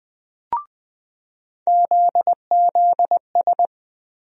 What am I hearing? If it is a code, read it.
Morse code, heard as a keyed tone, about 20 wpm, 700 Hz. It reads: ZZS
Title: USA Call Sign Postfixes - 3 Letters 20wpm